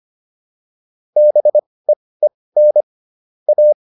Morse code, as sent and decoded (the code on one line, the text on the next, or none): -... . -. / .-
BEEN A